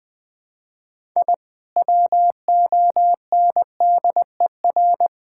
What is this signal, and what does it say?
Morse code, heard as a keyed tone, about 20 wpm, 700 Hz. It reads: I WONDER
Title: Sentences with 2-Word Phrases - 20wpm